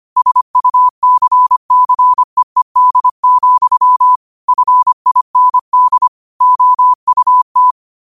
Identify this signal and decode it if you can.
Morse code, heard as a keyed tone, about 25 wpm, 1000 Hz. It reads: SUCCEED, FIND OUT
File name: qotdfiles / 25wpm